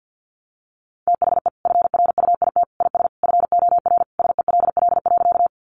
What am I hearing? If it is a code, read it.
Morse code, heard as a keyed tone, about 50 wpm, 700 Hz. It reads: THE FRUIT IS FOR SELL.
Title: Sentences from Top 1300 Words 50wpm